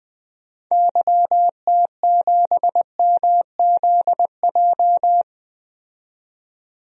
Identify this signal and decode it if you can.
Morse code, heard as a keyed tone, about 20 wpm, 700 Hz. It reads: YT7MZJ
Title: DXCC-100 Call Signs - 6 Characters - 20wpm